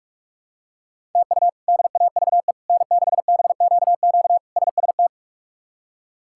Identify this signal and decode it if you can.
Morse code, heard as a keyed tone, about 46 wpm, 700 Hz. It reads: TU DAVE N6BQQ SST
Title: K1USN SST QSO Practice - 45wpm